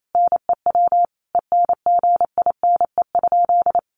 Morse code, heard as a keyed tone, about 28 wpm, 700 Hz. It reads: NEW ENGINE?
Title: Sentences from Top 1500 Words 28wpm